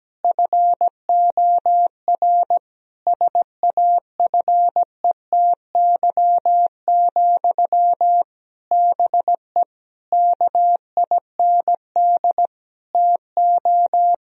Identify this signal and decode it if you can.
Morse code, heard as a keyed tone, about 17 wpm, 700 Hz. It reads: FOR SAFETY, BE KIND TO
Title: Sentences from Top 1000 Words - Review Part 4 - 17wpm